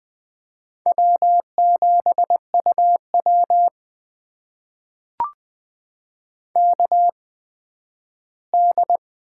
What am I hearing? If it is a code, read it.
Morse code, heard as a keyed tone, about 20 wpm, spaced as for 12 wpm, 700 Hz. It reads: W7UW KD
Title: USA Call Signs - 4 Characters - 8x Character Spacing - 20wpm